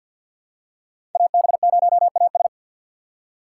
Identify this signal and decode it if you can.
Morse code, heard as a keyed tone, about 46 wpm, 700 Hz. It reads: AB0AS
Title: USA Call Signs - 3x Character Spacing - 50wpm